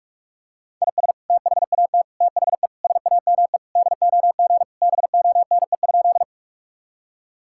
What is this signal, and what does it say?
Morse code, heard as a keyed tone, about 46 wpm, 700 Hz. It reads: IS THAT THE SAME DOG BONE?
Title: Sentences from Top 1600 Words 45wpm